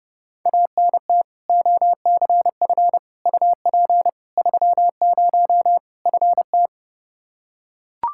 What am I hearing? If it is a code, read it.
Morse code, heard as a keyed tone, about 30 wpm, 700 Hz. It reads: ANT OCF UP 30 FT
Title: Practice QSO - Exchange 2 - Info - 30wpm